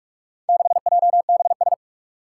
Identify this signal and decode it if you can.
Morse code, heard as a keyed tone, about 46 wpm, 700 Hz. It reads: 6JBS
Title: Four Challenging Letter-Number Combinations 45wpm